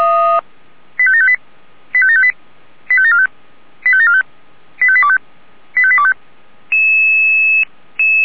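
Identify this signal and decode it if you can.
Five-tone selective call (ZVEI), heard as quick runs of five tones, tones of about 70 ms. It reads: 85758 85758 85745 85745 85825 85825
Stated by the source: ZVEI five-tone call received by SDR.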